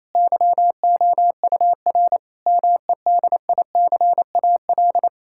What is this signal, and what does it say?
Morse code, heard as a keyed tone, about 28 wpm, 700 Hz. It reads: YOUR MEDICAL
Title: Sentences from Top 1000 Words - Review Part 5 - 28wpm